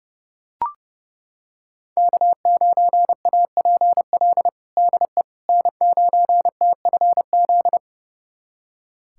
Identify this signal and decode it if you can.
Morse code, heard as a keyed tone, about 30 wpm, 700 Hz. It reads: K9APL DE N9TFZ
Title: USA Callsign DE Callsign - 5 Characters - 30wpm